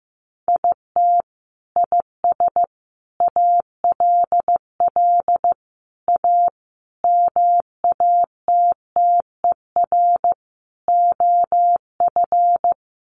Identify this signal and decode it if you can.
Morse code, heard as a keyed tone, about 15 wpm, 700 Hz. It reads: IT IS ALL A MATTER OF
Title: Sentences from Top 1300 Words 15wpm